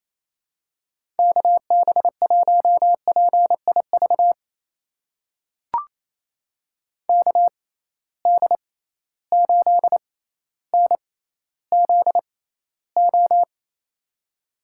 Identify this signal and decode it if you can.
Morse code, heard as a keyed tone, about 28 wpm, spaced as for 14 wpm, 700 Hz. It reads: KB1PIV KD8NZO
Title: USA Call Signs - 6 Characters - 6x Character Spacing - 28wpm